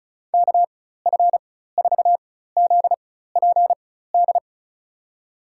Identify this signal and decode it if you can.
Morse code, heard as a keyed tone, about 36 wpm, spaced as for 16 wpm, 700 Hz. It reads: KF4ZPD
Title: USA Call Signs - 4x Character Spacing - 35wpm